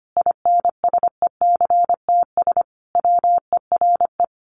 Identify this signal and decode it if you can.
Morse code, heard as a keyed tone, about 25 wpm, 700 Hz. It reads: INSECTS WERE
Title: A Sound Of Thunder 25wpm